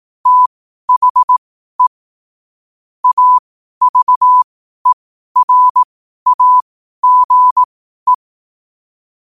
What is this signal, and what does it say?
Morse code, heard as a keyed tone, about 18 wpm, spaced as for 12 wpm, 1000 Hz. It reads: THE AVERAGE
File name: qotdfiles / 12wpm